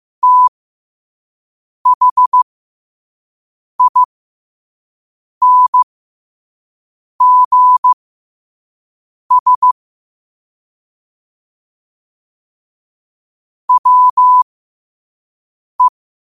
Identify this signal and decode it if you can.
Morse code, heard as a keyed tone, about 15 wpm, spaced as for 5 wpm, 1000 Hz. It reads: THINGS WE
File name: qotdfiles / 5wpm